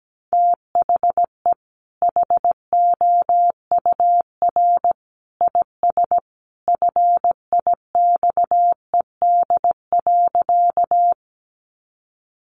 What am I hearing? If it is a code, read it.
Morse code, heard as a keyed tone, about 17 wpm, 700 Hz. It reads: THE HOUR IS FIXED.
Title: Sentences from Top 1100 Words 17wpm